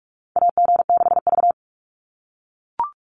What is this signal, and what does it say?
Morse code, heard as a keyed tone, about 44 wpm, 700 Hz. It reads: AG6V
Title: NA SOTA Chaser Callsigns 45wpm